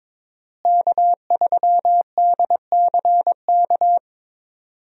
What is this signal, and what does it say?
Morse code, heard as a keyed tone, about 22 wpm, 700 Hz. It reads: K3DCK
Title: USA Call Signs - 5 Characters - Repeated 2x - 22wpm